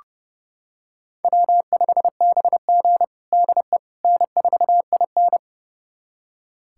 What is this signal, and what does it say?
Morse code, heard as a keyed tone, about 32 wpm, 700 Hz. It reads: W5BG DE N4IN
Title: USA Callsign DE Callsign - 4 Characters - 30wpm